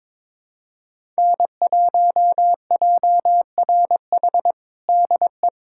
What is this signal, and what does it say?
Morse code, heard as a keyed tone, about 22 wpm, 700 Hz. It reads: N1JRH DE
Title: USA Callsign DE Callsign - 5 Characters - 22wpm